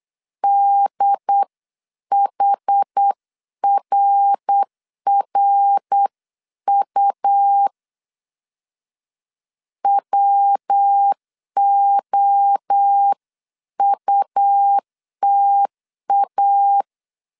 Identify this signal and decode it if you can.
Morse code, heard as a keyed tone, about 8 wpm, 800 Hz. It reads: DHRRU WOUTA
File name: CW / DET / 1 3 1500C S C 8wpm 4214